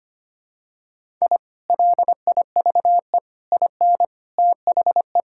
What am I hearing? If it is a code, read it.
Morse code, heard as a keyed tone, about 25 wpm, 700 Hz. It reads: I LIVE IN THE
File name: practice / 25wpm / Sentences with 2-Word Phrases - 25wpm